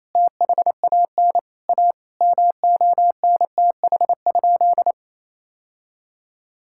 Morse code, heard as a keyed tone, about 28 wpm, 700 Hz. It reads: THAN A MONTH?
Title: Sentences from Top 1000 Words - Review Part 6 - 28wpm